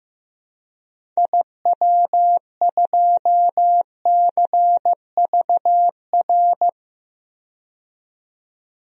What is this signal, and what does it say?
Morse code, heard as a keyed tone, about 15 wpm, 700 Hz. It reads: IW2CVR
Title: DXCC-100 Call Signs - 6 Characters - 15wpm